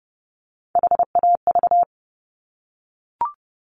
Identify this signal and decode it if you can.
Morse code, heard as a keyed tone, about 30 wpm, 700 Hz. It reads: HAV
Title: Top Three Letter Combinations in Words 30wpm